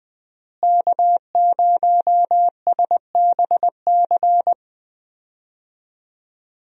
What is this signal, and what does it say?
Morse code, heard as a keyed tone, about 20 wpm, 700 Hz. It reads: K0SBC
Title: USA Call Signs - 5 Characters - Repeated 3x - 20wpm